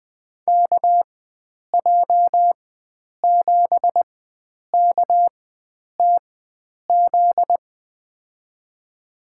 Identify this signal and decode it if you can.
Morse code, heard as a keyed tone, about 20 wpm, spaced as for 9 wpm, 700 Hz. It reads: KJ7KTZ
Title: USA Call Signs - 6 Characters - 4x Character Spacing - 20wpm